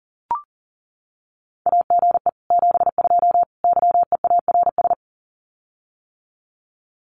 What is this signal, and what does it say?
Morse code, heard as a keyed tone, about 37 wpm, 700 Hz. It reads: AGE 72 YEARS
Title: Common QSO Phrases Encore 40wpm